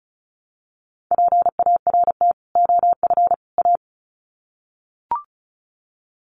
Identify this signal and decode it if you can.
Morse code, heard as a keyed tone, about 34 wpm, 700 Hz. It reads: PART OF A
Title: Sets of 3 Words 35wpm